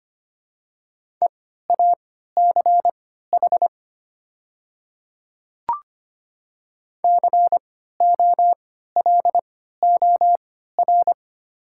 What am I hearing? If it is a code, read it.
Morse code, heard as a keyed tone, about 25 wpm, spaced as for 14 wpm, 700 Hz. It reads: EACH COLOR
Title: 3x Character Spacing - Top 300 Words - 25wpm